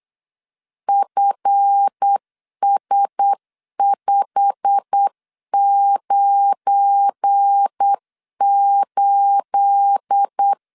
Morse code, heard as a keyed tone, about 8 wpm, 800 Hz. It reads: FS598
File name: LEZIONI / 9 28 8wpm 3512